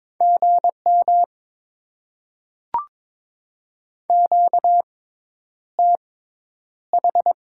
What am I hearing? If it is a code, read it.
Morse code, heard as a keyed tone, about 22 wpm, spaced as for 10 wpm, 690 Hz. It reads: GM QTH
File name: Top 20 QSO Elements - 6x Character Spacing - 22wpm